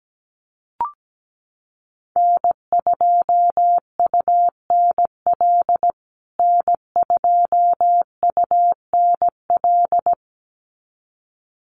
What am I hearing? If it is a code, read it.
Morse code, heard as a keyed tone, about 17 wpm, 700 Hz. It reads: N2UNL N2UNL
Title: USA Call Signs Repeated 2x 17wpm